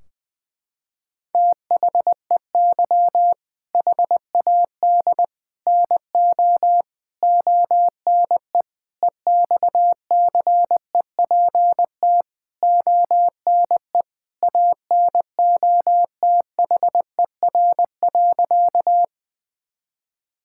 Morse code, heard as a keyed tone, about 21 wpm, 700 Hz. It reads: THEY HAD NO ONE EXCEPT ONE ANOTHER.
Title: Sentences from Top 1600 Words 20wpm